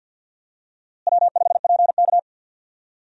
Morse code, heard as a keyed tone, about 51 wpm, 700 Hz. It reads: W5PK
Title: USA Call Signs - 4 Characters - Repeated 3x - 50wpm